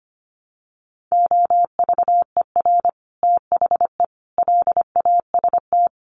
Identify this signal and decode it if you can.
Morse code, heard as a keyed tone, about 25 wpm, 700 Hz. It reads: OVER THE LAST
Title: Sets of 5 Words Encore 25wpm